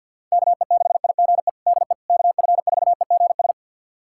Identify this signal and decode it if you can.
Morse code, heard as a keyed tone, about 51 wpm, 700 Hz. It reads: KE6IGE DE KF4EGS
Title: USA Callsign DE Callsign - 6 Characters - 50wpm